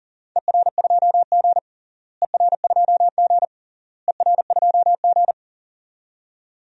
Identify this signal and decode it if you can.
Morse code, heard as a keyed tone, about 40 wpm, 700 Hz. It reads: ER2G ER2G ER2G
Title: DXCC-100 Call Signs - 4 Characters - Repeated 3x - 40wpm